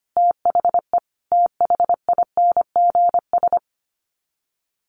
Morse code, heard as a keyed tone, about 25 wpm, 700 Hz. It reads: THE THINGS
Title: Sets of 5 Words 25wpm